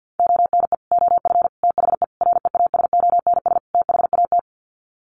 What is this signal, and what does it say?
Morse code, heard as a keyed tone, about 50 wpm, 700 Hz. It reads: ONE OF THE REASONS THAT
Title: Sets of 5 Words Encore 50wpm